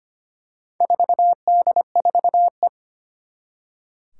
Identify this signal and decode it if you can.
Morse code, heard as a keyed tone, about 25 wpm, 700 Hz. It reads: 4D4E
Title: Four Challenging Letter-Number Combinations 25wpm